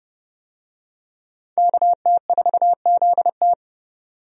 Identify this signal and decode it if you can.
Morse code, heard as a keyed tone, about 30 wpm, 700 Hz. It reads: KT4ZT
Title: USA Call Signs - 5 Characters - 30wpm